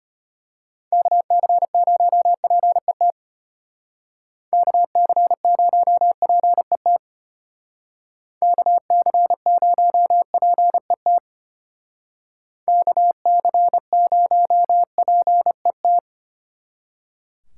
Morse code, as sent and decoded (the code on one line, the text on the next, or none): -.- -.-. ----- .--. - / -.- -.-. ----- .--. - / -.- -.-. ----- .--. - / -.- -.-. ----- .--. -
KC0PET KC0PET KC0PET KC0PET